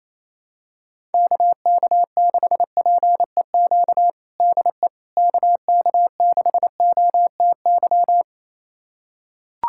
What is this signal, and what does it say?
Morse code, heard as a keyed tone, about 28 wpm, 700 Hz. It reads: KK6PEQ DE KK6OTY
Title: USA Callsign DE Callsign - 6 Characters - 28wpm